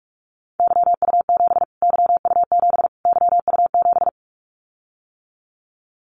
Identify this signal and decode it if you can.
Morse code, heard as a keyed tone, about 44 wpm, 700 Hz. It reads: YU7 YU7 YU7
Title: Callsign Prefixes for Top 25 DXCC - Repeated 3x - 45wpm